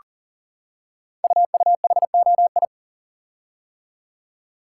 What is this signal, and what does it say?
Morse code, heard as a keyed tone, about 40 wpm, 700 Hz. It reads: UUHOI
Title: Code Groups - 5 Letters - 40wpm